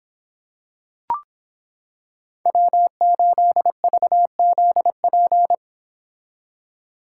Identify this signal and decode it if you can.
Morse code, heard as a keyed tone, about 26 wpm, 700 Hz. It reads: W8VZP